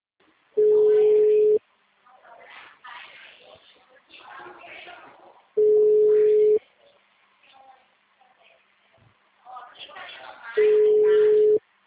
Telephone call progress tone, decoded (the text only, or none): ringback tone